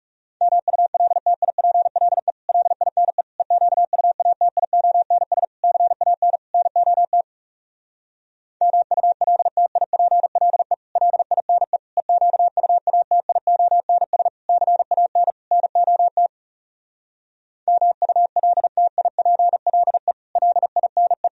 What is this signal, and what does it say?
Morse code, heard as a keyed tone, about 37 wpm, 700 Hz. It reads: MULTIPLE LINE EQUATIONS CAN NOT MULTIPLE LINE EQUATIONS CAN NOT MULTIPLE LINE
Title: Speed-Racing - Sets of 5 Words 30wpm